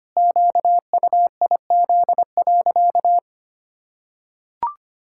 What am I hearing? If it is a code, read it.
Morse code, heard as a keyed tone, about 25 wpm, 700 Hz. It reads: QUIZ.
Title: Pangram Challenge v2 - 25wpm